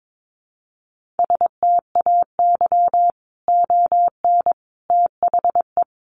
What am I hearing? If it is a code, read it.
Morse code, heard as a keyed tone, about 22 wpm, 700 Hz. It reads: STAY ON THE
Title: A Sound Of Thunder 22wpm